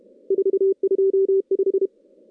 Morse code, heard as a keyed tone, about 32 wpm, 390 Hz. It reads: 425
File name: contests / CW-SS-k5tr- -0000